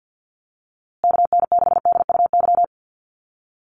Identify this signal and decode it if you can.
Morse code, heard as a keyed tone, about 50 wpm, 700 Hz. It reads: KN6DUY
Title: USA Call Signs Repeated 3x 50wpm